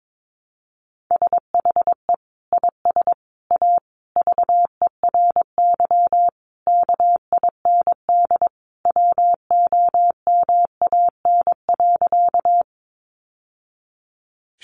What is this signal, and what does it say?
Morse code, heard as a keyed tone, about 22 wpm, 700 Hz. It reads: SHE IS A VERY KIND WOMAN.
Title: Sentences from Top 1100 Words 22wpm